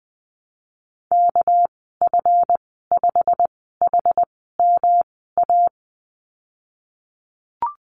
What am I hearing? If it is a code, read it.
Morse code, heard as a keyed tone, about 19 wpm, spaced as for 14 wpm, 700 Hz. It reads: KF5HMA
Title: 219 Lesson 35g - Number 0 - Call Signs 20wpm